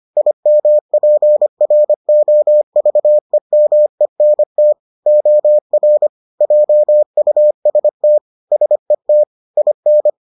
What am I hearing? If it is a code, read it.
Morse code, heard as a keyed tone, about 25 wpm, 600 Hz. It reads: IMPROVEMENT OR JUST SET IN